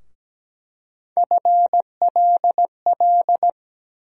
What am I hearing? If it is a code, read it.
Morse code, heard as a keyed tone, about 17 wpm, 700 Hz. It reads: FLL
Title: USA Call Sign Postfixes - 3 Letters 17wpm